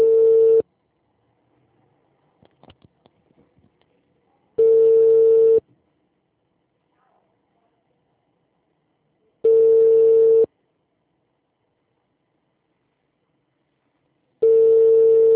Ringback tone (China, 450 Hz).
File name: out- -2013- - - .22594